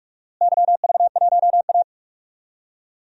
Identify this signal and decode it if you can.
Morse code, heard as a keyed tone, about 46 wpm, 700 Hz. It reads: YV1A
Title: DXCC-100 Call Signs - 4 Characters - Repeated 2x - 45wpm